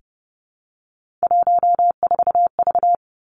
Morse code, heard as a keyed tone, about 30 wpm, 700 Hz. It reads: 14V
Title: Three Challenging Letter-Number Combinations 30wpm